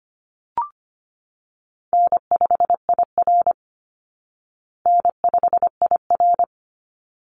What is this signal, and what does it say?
Morse code, heard as a keyed tone, about 25 wpm, 700 Hz. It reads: N5IR N5IR
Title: Call Signs Repeated 3x - 25wpm